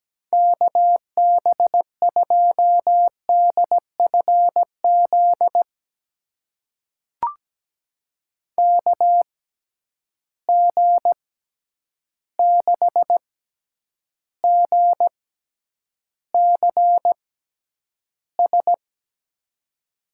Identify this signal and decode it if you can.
Morse code, heard as a keyed tone, about 17 wpm, spaced as for 9 wpm, 700 Hz. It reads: KB2DFZ KG6GCS
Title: USA Call Signs - 6 Characters - 6x Character Spacing - 17wpm